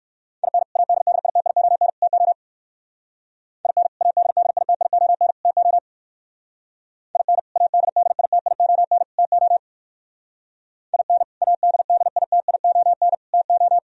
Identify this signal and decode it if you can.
Morse code, heard as a keyed tone, about 57 wpm, 700 Hz. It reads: IN ADDITION TO IN ADDITION TO IN ADDITION TO IN ADDITION TO